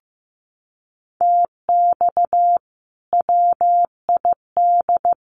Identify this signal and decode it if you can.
Morse code, heard as a keyed tone, about 15 wpm, 700 Hz. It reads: TX WID
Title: Common QSO Phrases 15wpm